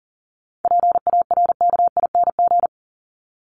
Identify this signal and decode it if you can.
Morse code, heard as a keyed tone, about 40 wpm, 700 Hz. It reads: PARKING
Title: Top 1501-1750 Words 40wpm